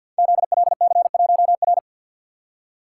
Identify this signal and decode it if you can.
Morse code, heard as a keyed tone, about 52 wpm, 700 Hz. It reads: ZLC1R
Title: Code Groups - 5 Letter:Numbers - 50wpm